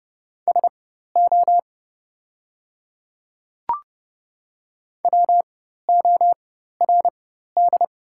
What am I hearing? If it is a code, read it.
Morse code, heard as a keyed tone, about 30 wpm, spaced as for 14 wpm, 700 Hz. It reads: SO WORD
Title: 4x Character Spacing - Top 100 Words - 30wpm